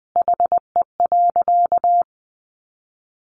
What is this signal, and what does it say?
Morse code, heard as a keyed tone, about 20 wpm, 700 Hz. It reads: HE.
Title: Sentences from Top 400 Words 20wpm